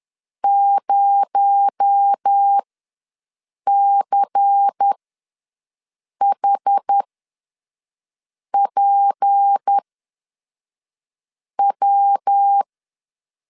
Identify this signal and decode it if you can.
Morse code, heard as a keyed tone, about 10 wpm, spaced as for 5 wpm, 800 Hz. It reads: 0CHPW
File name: LEZIONI / 10 SIGNS / 10A SLASH ws 10wpm 4121